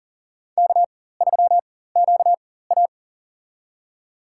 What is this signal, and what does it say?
Morse code, heard as a keyed tone, about 42 wpm, spaced as for 19 wpm, 700 Hz. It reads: K3QA
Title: USA Call Signs - 4 Characters - 4x Character Spacing - 40wpm